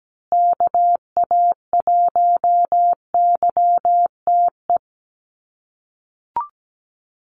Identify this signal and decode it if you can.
Morse code, heard as a keyed tone, about 17 wpm, 700 Hz. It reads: KA1YTE
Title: USA Call Signs Repeated 3x 17wpm